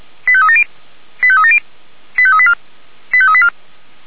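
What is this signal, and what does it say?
Five-tone selective call (ZVEI), heard as quick runs of five tones, tones of about 70 ms. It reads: 85370 85370 85384 85384
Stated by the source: ZVEI five-tone selective call received on SDR.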